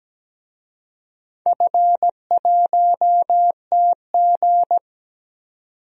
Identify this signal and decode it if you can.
Morse code, heard as a keyed tone, about 17 wpm, 700 Hz. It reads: F1TG
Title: DXCC-100 Call Signs - 4 Characters - 17wpm